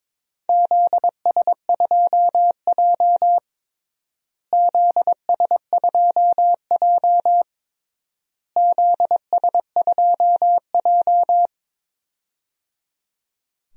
Morse code, heard as a keyed tone, about 22 wpm, 700 Hz. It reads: ZS2J ZS2J ZS2J
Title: DXCC-100 Call Signs - 4 Characters - Repeated 3x - 22wpm